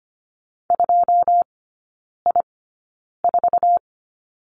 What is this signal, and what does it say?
Morse code, heard as a keyed tone, about 25 wpm, spaced as for 9 wpm, 700 Hz. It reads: 2I4
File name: Lesson 17e - Letter u - Three Characters 25wpm